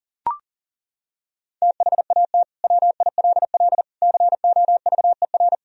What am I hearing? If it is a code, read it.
Morse code, heard as a keyed tone, about 40 wpm, 700 Hz. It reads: THAT WILL COVER
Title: Sentences from Top 1000 Words - Review Part 1 - 40wpm